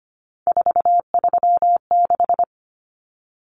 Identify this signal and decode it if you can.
Morse code, heard as a keyed tone, about 25 wpm, 700 Hz. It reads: 436
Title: Numbers - 3-Digit 25wpm